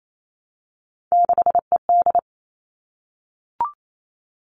Morse code, heard as a keyed tone, about 28 wpm, 700 Hz. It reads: 6ED
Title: All Cut Numbers - 3-Digit 28wpm